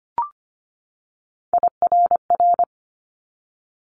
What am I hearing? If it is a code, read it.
Morse code, heard as a keyed tone, about 25 wpm, 700 Hz. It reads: IRR
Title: Multi-letter Phonemes 25wpm